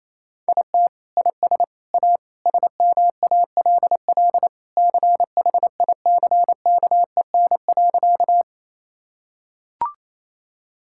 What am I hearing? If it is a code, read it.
Morse code, heard as a keyed tone, about 28 wpm, 700 Hz. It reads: IT IS A SMALL CHICKEN.
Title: Sentences from Top 1000 Words - Review Part 5 - 28wpm